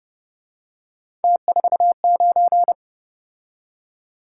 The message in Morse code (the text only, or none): T49